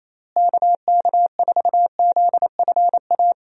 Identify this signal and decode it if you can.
Morse code, heard as a keyed tone, about 28 wpm, 700 Hz. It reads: KK4ZFA
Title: USA Call Signs - 6 Characters - Repeated 3x - 28wpm